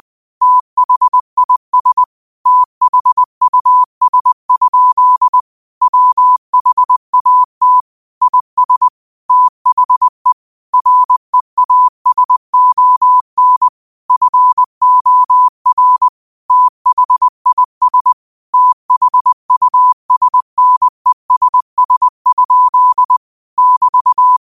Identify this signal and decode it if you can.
Morse code, heard as a keyed tone, about 20 wpm, 1000 Hz. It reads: THIS THUS? WHAT IS THE REASON FOR THIS THUSNESS? <BT>
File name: qotdfiles / 20wpm